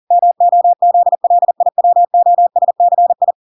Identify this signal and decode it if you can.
Morse code, heard as a keyed tone, about 40 wpm, 700 Hz. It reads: MOZLIWOSCI